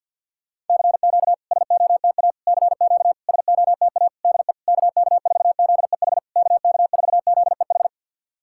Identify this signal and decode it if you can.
Morse code, heard as a keyed tone, about 51 wpm, 690 Hz. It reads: CQ SOTA CQ SOTA DE KK4ZEH KK4ZEH
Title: SOTA QSO Practice 50wpm